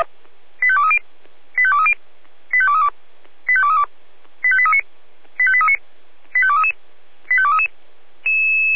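Five-tone selective call (ZVEI), heard as quick runs of five tones, tones of about 70 ms. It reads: 85230 85230 85232 85232 85839 85839 85233 85233
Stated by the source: ZVEI five-tone selective call received on an SDR.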